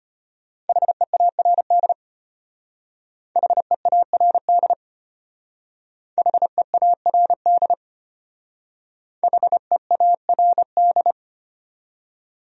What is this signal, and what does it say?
Morse code, heard as a keyed tone, about 32 wpm, 700 Hz. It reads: HEARD HEARD HEARD HEARD